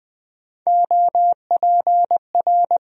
Morse code, heard as a keyed tone, about 20 wpm, 700 Hz. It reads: OPR
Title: USA Call Sign Postfixes - 3 Letters 20wpm